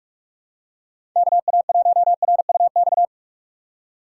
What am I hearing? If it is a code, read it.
Morse code, heard as a keyed tone, about 45 wpm, 700 Hz. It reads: KA1RUX
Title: USA Call Signs - 2x Character Spacing - 45wpm